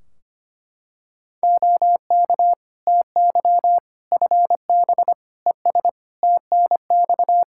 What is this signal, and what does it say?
Morse code, heard as a keyed tone, about 25 wpm, 700 Hz. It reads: OK TY FB ES TNX
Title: K1USN SST QSO Practice 25wpm